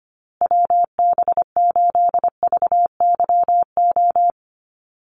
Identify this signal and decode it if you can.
Morse code, heard as a keyed tone, about 25 wpm, 700 Hz. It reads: WB8VYO
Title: USA Call Signs Repeated 3x 25wpm